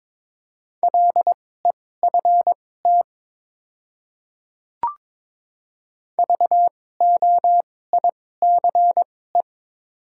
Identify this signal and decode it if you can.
Morse code, heard as a keyed tone, about 22 wpm, spaced as for 16 wpm, 690 Hz. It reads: LEFT VOICE
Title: 2x Character Spacing - Top 500 Words - 22wpm